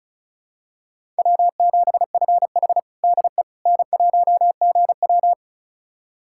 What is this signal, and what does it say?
Morse code, heard as a keyed tone, about 36 wpm, 700 Hz. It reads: W7FH DE N1GW
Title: USA Callsign DE Callsign - 4 Characters - 35wpm